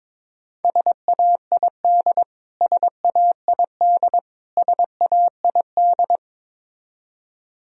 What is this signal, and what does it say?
Morse code, heard as a keyed tone, about 22 wpm, 700 Hz. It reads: SAID SAID SAID
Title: Top 100 Words Repeated 22wpm